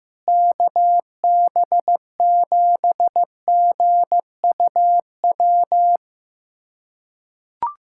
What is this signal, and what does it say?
Morse code, heard as a keyed tone, about 15 wpm, 700 Hz. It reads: KB7GUW
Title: USA Callsign DE Callsign - 6 Characters - 15wpm